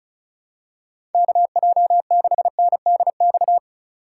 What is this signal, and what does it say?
Morse code, heard as a keyed tone, about 34 wpm, 700 Hz. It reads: KJ6NDX
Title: USA Call Signs - 6 Characters - 3x Character Spacing - 35wpm